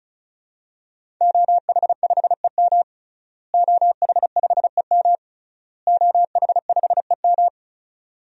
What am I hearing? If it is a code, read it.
Morse code, heard as a keyed tone, about 34 wpm, 700 Hz. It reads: OH5EM OH5EM OH5EM
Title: DXCC-100 Call Signs - 5 Characters - Repeated 3x - 35wpm